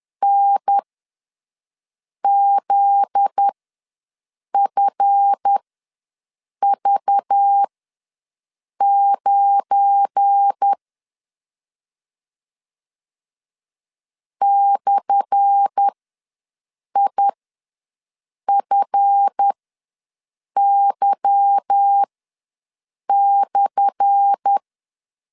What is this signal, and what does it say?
Morse code, heard as a keyed tone, about 10 wpm, spaced as for 6 wpm, 800 Hz. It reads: NZFV9 /IFY/
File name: LEZIONI / 10 SIGNS / 10A SLASH ws 10wpm 4121